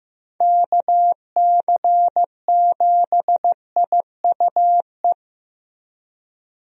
Morse code, heard as a keyed tone, about 15 wpm, 700 Hz. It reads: KC7IUE
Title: USA Call Signs - 6 Characters - Repeated 3x - 15wpm